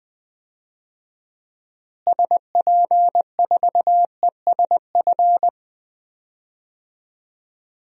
Morse code, heard as a keyed tone, about 20 wpm, 700 Hz. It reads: SP4ESF
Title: DXCC-100 Call Signs - 6 Characters - Repeated 2x - 20wpm